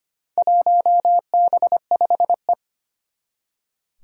Morse code, heard as a keyed tone, about 25 wpm, 700 Hz. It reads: 1B5E
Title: Four Challenging Letter-Number Combinations 25wpm